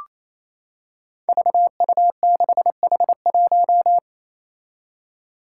Morse code, heard as a keyed tone, about 28 wpm, 700 Hz. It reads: VU6H1
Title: Five Challenging Letter-Number Combinations 28wpm